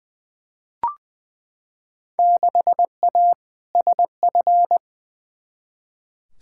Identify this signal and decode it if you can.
Morse code, heard as a keyed tone, about 20 wpm, 700 Hz. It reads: 6A SF
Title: Field Day Exchanges - 20wpm